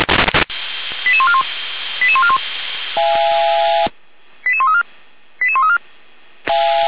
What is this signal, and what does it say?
Five-tone selective call (ZVEI), heard as quick runs of five tones, tones of about 70 ms. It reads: 88141 88141 88134 88134